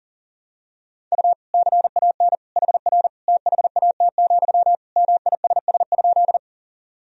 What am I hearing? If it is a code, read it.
Morse code, heard as a keyed tone, about 40 wpm, 700 Hz. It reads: U CAN HR THAT, MISS?
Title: Sentences from Top 500 Words and Abbreviations 40wpm